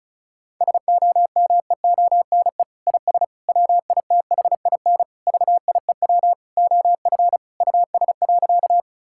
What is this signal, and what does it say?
Morse code, heard as a keyed tone, about 34 wpm, 700 Hz. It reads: SOMEONE IS WITHIN VIEW OF US.
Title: Sentences from Top 1000 Words - Review Part 6 - 35wpm